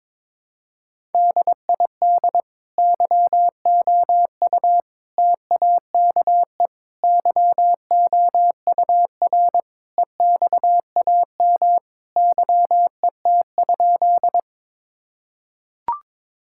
Morse code, heard as a keyed tone, about 22 wpm, 700 Hz. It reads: DID YOU TAKE YOUR EXAM YET?
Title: Sentences from Top 1000 Words - Review Part 5 - 22wpm